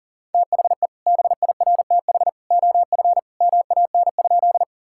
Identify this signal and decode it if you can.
Morse code, heard as a keyed tone, about 40 wpm, 700 Hz. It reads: THE BIRTH OF MAN?
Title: Sentences from Top 1000 Words - Review Part 5 - 40wpm